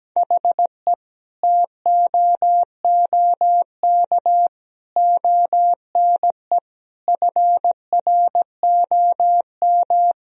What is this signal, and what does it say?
Morse code, heard as a keyed tone, about 17 wpm, 700 Hz. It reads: HE TOOK ONE FROM
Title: Sentences from Top 1000 Words - Review Part 1 - 17wpm